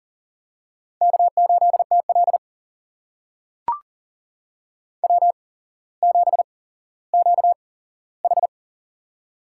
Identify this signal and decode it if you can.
Morse code, heard as a keyed tone, about 40 wpm, spaced as for 17 wpm, 700 Hz. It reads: K8TL W7QH
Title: USA Call Signs - 4 Characters - 8x Character Spacing - 40wpm